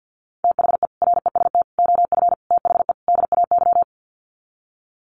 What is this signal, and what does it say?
Morse code, heard as a keyed tone, about 50 wpm, 700 Hz. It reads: THE REST OF THE DAY
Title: Sets of 5 Words 50wpm